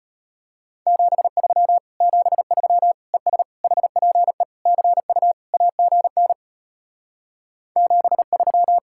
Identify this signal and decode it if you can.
Morse code, heard as a keyed tone, about 33 wpm, 700 Hz. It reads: 73 73 ES HPE CU AGN 73